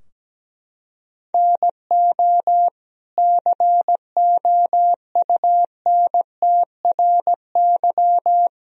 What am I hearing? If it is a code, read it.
Morse code, heard as a keyed tone, about 17 wpm, 700 Hz. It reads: NO COUNTRY
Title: Sentences from Top 1000 Words - Review Part 3 - 17wpm